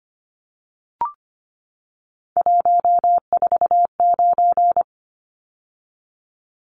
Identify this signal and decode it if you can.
Morse code, heard as a keyed tone, about 25 wpm, 700 Hz. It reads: 149
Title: Numbers - 3-Digit 25wpm